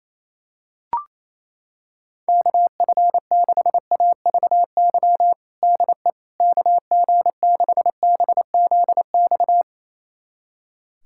Morse code, heard as a keyed tone, about 28 wpm, 700 Hz. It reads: KF6AVY DE KG6BZX
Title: USA Callsign DE Callsign - 6 Characters - 28wpm